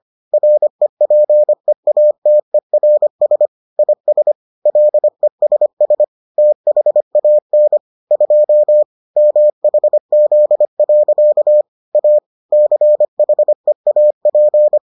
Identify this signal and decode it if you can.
Morse code, heard as a keyed tone, about 25 wpm, 600 Hz. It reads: REPEATERS IS LESS THAN 2 MHZ. A CHEAP